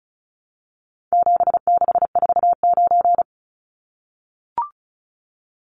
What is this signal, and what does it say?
Morse code, heard as a keyed tone, about 34 wpm, 700 Hz. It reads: 7649
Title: Numbers - 4-Digit 35wpm